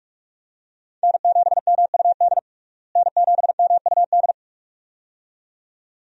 Morse code, heard as a keyed tone, about 46 wpm, 700 Hz. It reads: N7MUD N7MUD
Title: USA Call Signs - 5 Characters - Repeated 2x - 45wpm